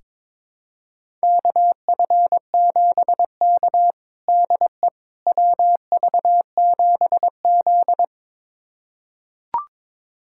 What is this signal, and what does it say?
Morse code, heard as a keyed tone, about 22 wpm, 700 Hz. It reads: KF7K DE WV7Z
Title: USA Callsign DE Callsign - 4 Characters - 22wpm